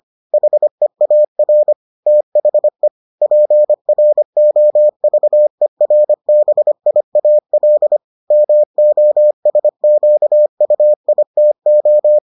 Morse code, heard as a keyed tone, about 25 wpm, 600 Hz. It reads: HEAR THE PROVERBIAL MOSQUITO